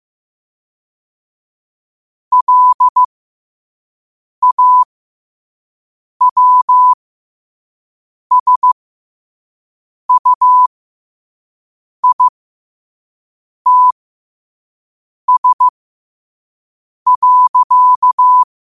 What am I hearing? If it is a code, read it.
Morse code, heard as a keyed tone, about 15 wpm, spaced as for 5 wpm, 1000 Hz. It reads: LAWSUITS.
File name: qotdfiles / 5wpm